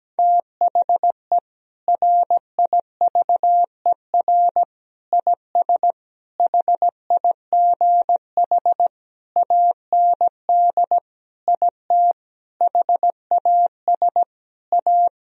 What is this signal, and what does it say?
Morse code, heard as a keyed tone, about 17 wpm, 700 Hz. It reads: THE RIVER IS HIGH AND IT HAS A
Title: Sentences from Top 1000 Words - Review Part 4 - 17wpm